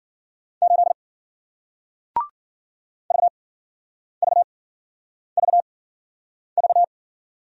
Morse code, heard as a keyed tone, about 56 wpm, spaced as for 9 wpm, 700 Hz. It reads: 8 VVVV